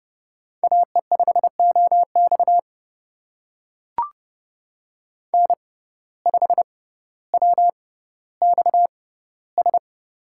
Morse code, heard as a keyed tone, about 30 wpm, spaced as for 15 wpm, 700 Hz. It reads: AE5OX N5WXS
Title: USA Call Signs - 5 Characters - 6x Character Spacing - 30wpm